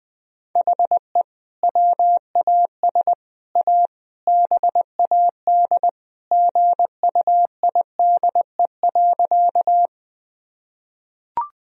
Morse code, heard as a keyed tone, about 20 wpm, 700 Hz. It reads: HE WAS A BAD GUIDE.
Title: Sentences from Top 1000 Words - Review Part 3 - 20wpm